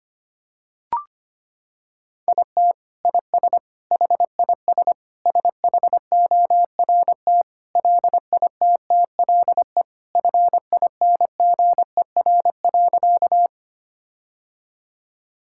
Morse code, heard as a keyed tone, about 25 wpm, 700 Hz. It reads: IT IS HIS SHORT LITTLE FINGER.
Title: Sentences from Top 1600 Words 25wpm